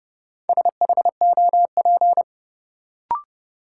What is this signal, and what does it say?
Morse code, heard as a keyed tone, about 32 wpm, 700 Hz. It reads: SHOP
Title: Sentences from Top 1600 Words - New Word Drill - 30wpm